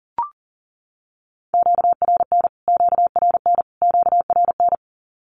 Morse code, heard as a keyed tone, about 40 wpm, 700 Hz. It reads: QRN QRN QRN
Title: Q-Codes Repeated 3x 40wpm